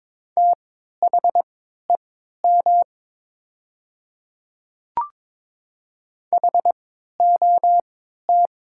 Morse code, heard as a keyed tone, about 22 wpm, spaced as for 13 wpm, 700 Hz. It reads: THEM HOT
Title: 3x Character Spacing - Top 300 Words - 22wpm